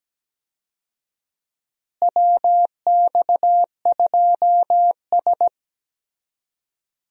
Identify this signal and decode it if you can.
Morse code, heard as a keyed tone, about 17 wpm, 700 Hz. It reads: WX2S
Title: USA Call Signs - 4 Characters - Repeated 3x - 17wpm